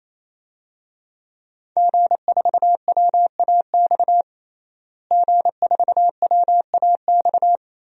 Morse code, heard as a keyed tone, about 28 wpm, 700 Hz. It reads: G4WAX G4WAX
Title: DXCC-100 Call Signs - 5 Characters - Repeated 3x - 28wpm